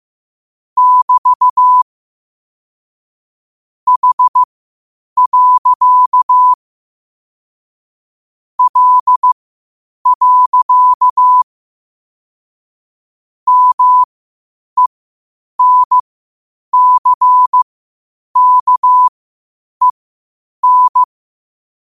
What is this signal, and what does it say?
Morse code, heard as a keyed tone, about 15 wpm, spaced as for 8 wpm, 1000 Hz. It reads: <BT> H. L. MENCKEN
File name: qotdfiles / 8wpm